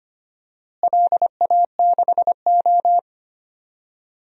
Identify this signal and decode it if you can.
Morse code, heard as a keyed tone, about 25 wpm, 700 Hz. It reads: LA6O
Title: DXCC-100 Call Signs - 4 Characters - Repeated 2x - 25wpm